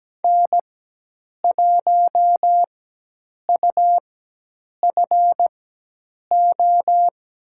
Morse code, heard as a keyed tone, about 17 wpm, spaced as for 8 wpm, 700 Hz. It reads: N1UFO
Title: USA Call Signs - 5 Characters - 4x Character Spacing - 17wpm